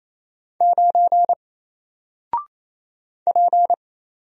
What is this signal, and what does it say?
Morse code, heard as a keyed tone, about 27 wpm, 700 Hz. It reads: 9 P
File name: Single-Letter-Number-28wpm